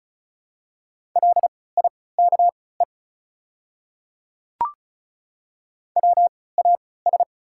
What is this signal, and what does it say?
Morse code, heard as a keyed tone, about 36 wpm, spaced as for 20 wpm, 690 Hz. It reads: LIKE WAS
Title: 3x Character Spacing - Top 100 Words - 35wpm